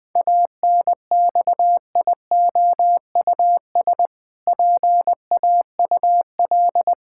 Morse code, heard as a keyed tone, about 20 wpm, 700 Hz. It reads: ANXIOUS PAUL
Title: Pangram Challenge v2 - 20wpm